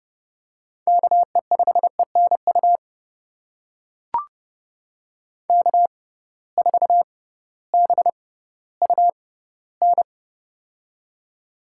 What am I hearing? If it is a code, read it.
Morse code, heard as a keyed tone, about 30 wpm, spaced as for 16 wpm, 700 Hz. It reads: KE5ENU K4BUN
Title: USA Call Signs - 6x Character Spacing - 30wpm